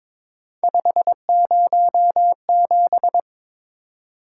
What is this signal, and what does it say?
Morse code, heard as a keyed tone, about 22 wpm, 700 Hz. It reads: 507
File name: practice / 22wpm / Three Letter-Number Combinations - 22wpm